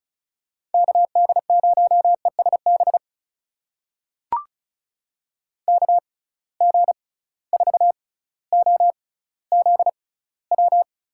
Morse code, heard as a keyed tone, about 36 wpm, spaced as for 18 wpm, 700 Hz. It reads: KD0ESB KG4OZW
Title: USA Call Signs - 6 Characters - 6x Character Spacing - 35wpm